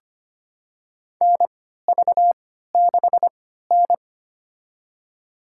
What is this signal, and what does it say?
Morse code, heard as a keyed tone, about 25 wpm, spaced as for 14 wpm, 700 Hz. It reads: NV6N
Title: USA Call Signs - 4 Characters - 3x Character Spacing - 25wpm